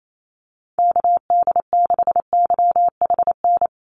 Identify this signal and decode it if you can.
Morse code, heard as a keyed tone, about 28 wpm, 700 Hz. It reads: KD6YHN
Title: USA Call Signs 28wpm